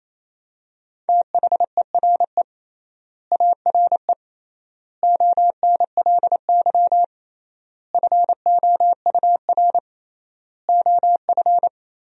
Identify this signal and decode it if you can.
Morse code, heard as a keyed tone, about 28 wpm, 700 Hz. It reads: THERE ARE ONLY FOUR OF
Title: 3x Word-Spacing - Sentences from Top 300 Words - 28wpm